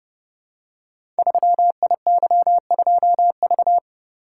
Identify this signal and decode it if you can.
Morse code, heard as a keyed tone, about 30 wpm, 700 Hz. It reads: 3IY2V
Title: Code Groups - 5 Letter:Numbers - 30wpm